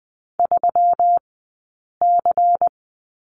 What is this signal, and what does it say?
Morse code, heard as a keyed tone, about 20 wpm, spaced as for 8 wpm, 700 Hz. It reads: 3C
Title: 165 Lesson 27d - Number 7 - Two Characters 20wpm